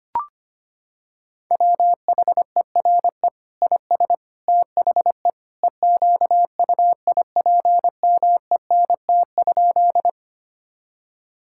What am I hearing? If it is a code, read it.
Morse code, heard as a keyed tone, about 25 wpm, 700 Hz. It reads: WHERE IS THE EQUIPMENT?
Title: Sentences from Top 1000 Words - Review Part 5 - 25wpm